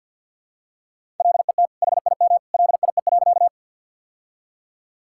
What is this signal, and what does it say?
Morse code, heard as a keyed tone, about 50 wpm, 700 Hz. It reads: LET HIM LIE.
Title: Sentences from Top 1600 Words 50wpm